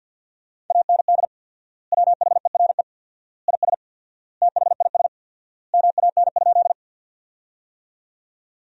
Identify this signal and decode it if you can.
Morse code, heard as a keyed tone, about 50 wpm, 700 Hz. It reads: AND WHERE IS THIS MAN?
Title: 4x Word-Spacing - Sentences from Top 200 Words - 50wpm